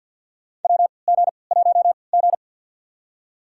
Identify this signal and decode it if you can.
Morse code, heard as a keyed tone, about 48 wpm, spaced as for 28 wpm, 700 Hz. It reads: WG1G
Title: USA Call Signs - 4 Characters - 3x Character Spacing - 50wpm